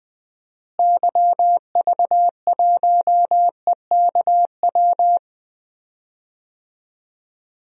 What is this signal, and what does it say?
Morse code, heard as a keyed tone, about 20 wpm, 700 Hz. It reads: YV1EKW
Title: DXCC-100 Call Signs - 6 Characters - Repeated 3x - 20wpm